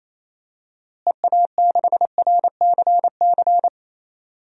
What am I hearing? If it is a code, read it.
Morse code, heard as a keyed tone, about 28 wpm, 700 Hz. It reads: EA6RCC
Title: DXCC-100 Call Signs - 6 Characters - Repeated 3x - 28wpm